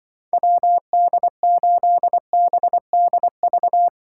Morse code, heard as a keyed tone, about 24 wpm, 700 Hz. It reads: WD8BDV
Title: Speed Racing - USA 2x3 Call Signs 20wpm